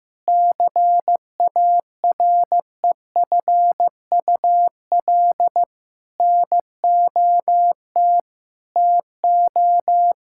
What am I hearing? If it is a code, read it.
Morse code, heard as a keyed tone, about 15 wpm, 700 Hz. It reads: CAREFUL NOT TO
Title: Sentences from Top 1600 Words 15wpm